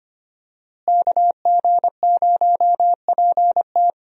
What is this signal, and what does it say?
Morse code, heard as a keyed tone, about 25 wpm, 700 Hz. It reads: KG0PT
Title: USA Call Signs - 5 Characters - Repeated 3x - 25wpm